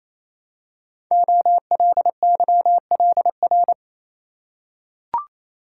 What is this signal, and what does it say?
Morse code, heard as a keyed tone, about 27 wpm, 700 Hz. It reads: OLYLR
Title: Code Groups - 5 Letter:Numbers - 28wpm